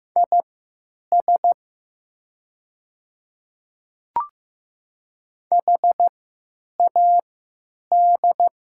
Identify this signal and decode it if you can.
Morse code, heard as a keyed tone, about 15 wpm, spaced as for 8 wpm, 700 Hz. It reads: IS HAD
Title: 3x Character Spacing - Top 100 Words - 15wpm